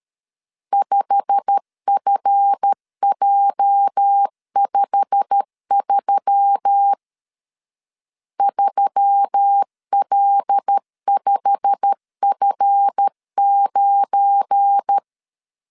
Morse code, heard as a keyed tone, about 13 wpm, 800 Hz. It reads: 5FJ53 3L5F9
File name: LEZIONI / 8 YJ45 / 8 YJ45 12wpm 2151